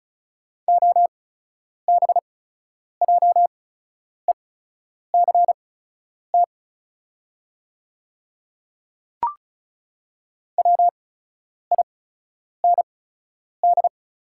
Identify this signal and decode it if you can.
Morse code, heard as a keyed tone, about 34 wpm, spaced as for 10 wpm, 690 Hz. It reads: OBJECT WIND
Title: 8x Character Spacing - Top 500 Words - 35wpm